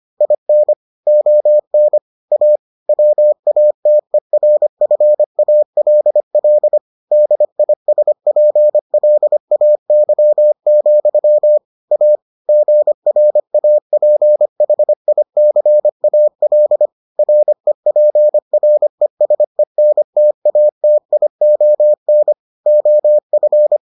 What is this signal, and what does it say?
Morse code, heard as a keyed tone, about 25 wpm, 600 Hz. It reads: IN ON A WATERFALL DISPLAY, A GRAPHICAL REPRESENTATION OF